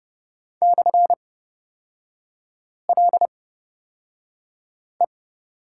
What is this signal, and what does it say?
Morse code, heard as a keyed tone, about 30 wpm, spaced as for 5 wpm, 700 Hz. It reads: /LE
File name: Lesson 37e - Letter x - Three Characters 30wpm